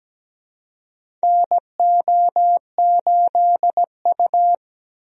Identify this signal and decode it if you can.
Morse code, heard as a keyed tone, about 17 wpm, 700 Hz. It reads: NO8U
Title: USA Call Signs - 4 Characters - Repeated 3x - 17wpm